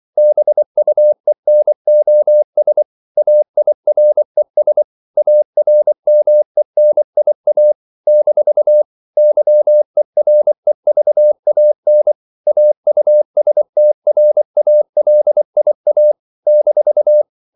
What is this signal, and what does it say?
Morse code, heard as a keyed tone, about 24 wpm, 600 Hz. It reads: BUENOS AIRES ARMENIA - YEREVAN AUSTRALIA -
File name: CW / Practice / NationalCapitals24